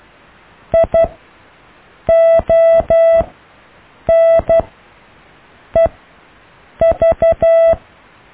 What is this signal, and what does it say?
Morse code, heard as a keyed tone, about 12 wpm, spaced as for 6 wpm, 660 Hz. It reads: IONEV
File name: cw / IJLUM+CDRO+EKTPV+BFHNW